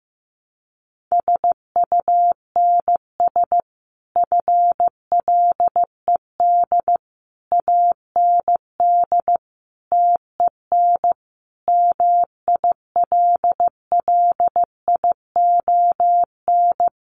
Morse code, heard as a keyed tone, about 15 wpm, 700 Hz. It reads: SUNS FLED AND TEN MILLION
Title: A Sound Of Thunder 15wpm